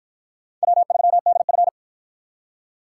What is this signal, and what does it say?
Morse code, heard as a keyed tone, about 50 wpm, 700 Hz. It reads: W3DF